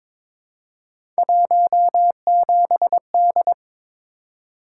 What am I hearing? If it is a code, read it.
Morse code, heard as a keyed tone, about 22 wpm, 700 Hz. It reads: 17D
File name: practice / 22wpm / Three Letter-Number Combinations - 22wpm